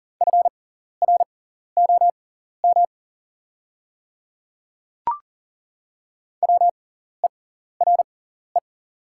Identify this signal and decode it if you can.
Morse code, heard as a keyed tone, about 40 wpm, spaced as for 14 wpm, 700 Hz. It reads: FROM WERE
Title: 6x Character Spacing - Top 100 Words - 40wpm